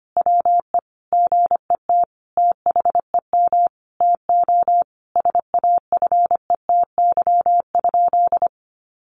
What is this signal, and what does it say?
Morse code, heard as a keyed tone, about 25 wpm, 700 Hz. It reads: WE GET THEM TO SAFETY?
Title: Sentences from Top 700 Words 25wpm